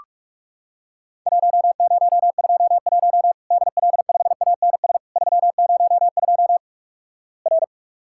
Morse code, heard as a keyed tone, about 46 wpm, 700 Hz. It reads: 1021 DL5ANS 302 R
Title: Worked All Europe DX Contest - QTC Exchange - 45wpm